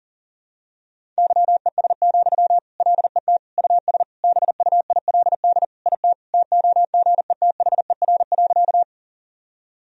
Morse code, heard as a keyed tone, about 37 wpm, 700 Hz. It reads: YES, LET US BUILD IT TOGETHER.
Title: Sentences from Top 1000 Words - Review Part 1 - 40wpm